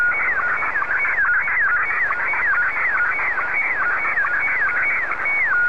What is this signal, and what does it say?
SSTV (slow-scan television), Scottie 1.